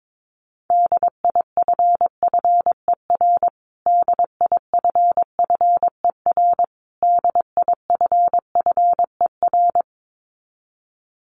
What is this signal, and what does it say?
Morse code, heard as a keyed tone, about 22 wpm, 700 Hz. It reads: DIFFER DIFFER DIFFER
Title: Top 200 Words Repeated 22wpm